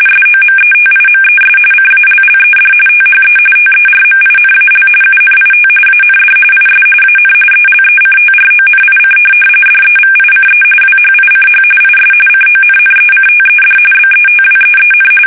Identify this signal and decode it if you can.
RTTY, about 45.45 baud. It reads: SOMETIMES CALLED THE TERMINAL UNIT AND IS AN ELECTRONIC DEVICE WHICH IS CONNECTED BETWEEN
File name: fsk850 / teletype-RTTY